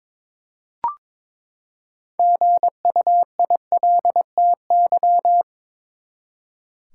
Morse code, heard as a keyed tone, about 22 wpm, 700 Hz. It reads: GUILTY
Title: Six Letter Words - 22wpm